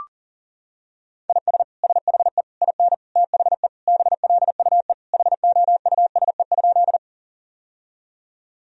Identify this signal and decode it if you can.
Morse code, heard as a keyed tone, about 40 wpm, 700 Hz. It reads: IS SHE IN THE BLUE HOUSE?
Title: Sentences from Top 1000 Words - Review Part 3 - 40wpm